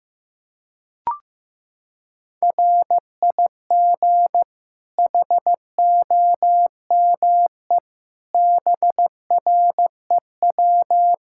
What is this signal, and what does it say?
Morse code, heard as a keyed tone, about 15 wpm, 700 Hz. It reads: RIG HOME BREW
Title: Practice QSO - Exchange 2 - Info - 15wpm